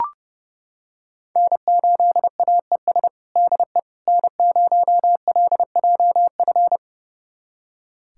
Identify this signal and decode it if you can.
Morse code, heard as a keyed tone, about 30 wpm, 700 Hz. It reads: N8AES DE N0LJF
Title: USA Callsign DE Callsign - 5 Characters - 30wpm